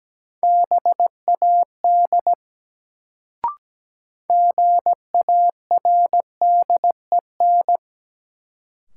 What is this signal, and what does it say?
Morse code, heard as a keyed tone, about 17 wpm, 700 Hz. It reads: BAD GARDEN
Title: Sentences from Top 700 Words - New Word Drill - 17wpm